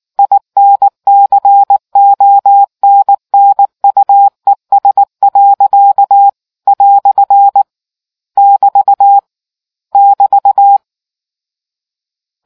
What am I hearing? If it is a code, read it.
Morse code, heard as a keyed tone, about 19 wpm, 800 Hz. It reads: INCONNUES. " <BT> <BT>